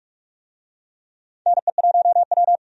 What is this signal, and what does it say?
Morse code, heard as a keyed tone, about 48 wpm, 700 Hz. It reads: NE1W
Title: USA Call Signs - 4 Characters - 45wpm